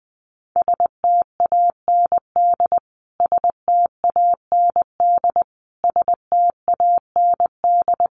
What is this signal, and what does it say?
Morse code, heard as a keyed tone, about 20 wpm, 700 Hz. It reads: STAND STAND STAND
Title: Top 500 Words Repeated 20wpm